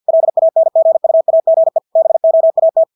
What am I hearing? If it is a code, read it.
Morse code, heard as a keyed tone, about 50 wpm, 650 Hz. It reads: LANGUAGE BOAT
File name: Engelske ord 50 wpm